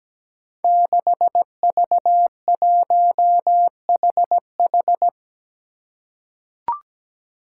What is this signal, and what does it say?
Morse code, heard as a keyed tone, about 17 wpm, 700 Hz. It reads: 6V1HH
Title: Five Challenging Letter-Number Combinations 17wpm